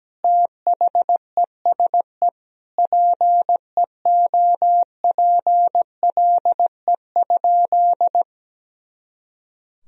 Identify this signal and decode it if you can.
Morse code, heard as a keyed tone, about 17 wpm, 700 Hz. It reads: THESE PEOPLE?
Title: Sentences from Top 200 Words and Abbreviations 17wpm